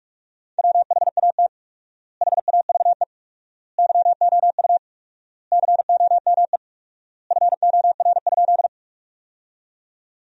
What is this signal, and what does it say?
Morse code, heard as a keyed tone, about 46 wpm, 700 Hz. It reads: WHAT HAVE YOU COME FOR?
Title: 4x Word-Spacing - Sentences from Top 100 Words - 45wpm